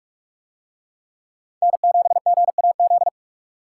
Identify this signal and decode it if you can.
Morse code, heard as a keyed tone, about 42 wpm, 700 Hz. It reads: N7GAZ